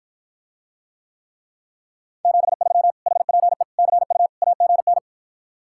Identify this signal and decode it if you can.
Morse code, heard as a keyed tone, about 54 wpm, 700 Hz. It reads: 73 HPE CU AGN